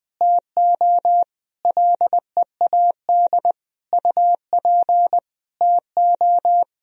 Morse code, heard as a keyed tone, about 20 wpm, 700 Hz. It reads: TO LEAD UP TO
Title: Sentences from Top 1000 Words - Review Part 2 - 20wpm